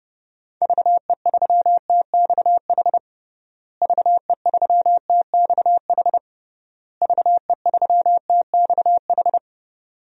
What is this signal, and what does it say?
Morse code, heard as a keyed tone, about 32 wpm, 700 Hz. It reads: VE3TXH VE3TXH VE3TXH
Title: DXCC-100 Call Signs - 6 Characters - Repeated 3x - 30wpm